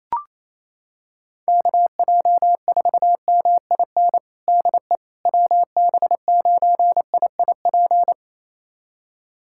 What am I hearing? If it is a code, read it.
Morse code, heard as a keyed tone, about 27 wpm, 700 Hz. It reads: KJ4MIN DE WB9IIP
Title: USA Callsign DE Callsign - 6 Characters - 28wpm